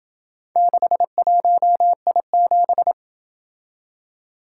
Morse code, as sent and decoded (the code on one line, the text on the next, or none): -.... .---- .. --...
61I7